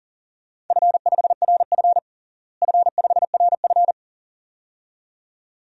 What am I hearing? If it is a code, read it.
Morse code, heard as a keyed tone, about 40 wpm, 700 Hz. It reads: F5RF F5RF
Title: DXCC-100 Call Signs - 4 Characters - Repeated 2x - 40wpm